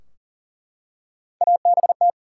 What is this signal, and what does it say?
Morse code, heard as a keyed tone, about 39 wpm, 700 Hz. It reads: ABT
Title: Top 90 QSO Elements 40wpm